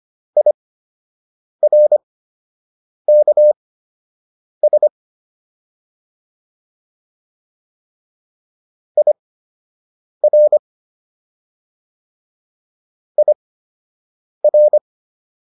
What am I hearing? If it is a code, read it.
Morse code, heard as a keyed tone, about 25 wpm, spaced as for 7 wpm, 600 Hz. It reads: IRKS IR IR